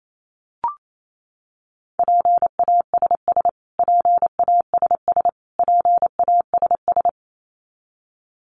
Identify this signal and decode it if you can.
Morse code, heard as a keyed tone, about 28 wpm, 700 Hz. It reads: PASS PASS PASS
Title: Top 500 Words Repeated 28wpm